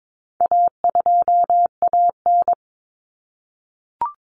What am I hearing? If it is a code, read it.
Morse code, heard as a keyed tone, about 22 wpm, 700 Hz. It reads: A2AN
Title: Common Cut Numbers - 4-Digit 22wpm